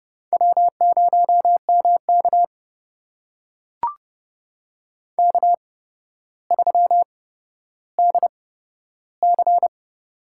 Morse code, heard as a keyed tone, about 31 wpm, spaced as for 13 wpm, 700 Hz. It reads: W0MK K3DC
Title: USA Call Signs - 4 Characters - 8x Character Spacing - 30wpm